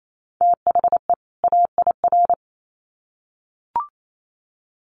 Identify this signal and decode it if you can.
Morse code, heard as a keyed tone, about 28 wpm, 700 Hz. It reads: THE AIR
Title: Sets of 2 Words 28wpm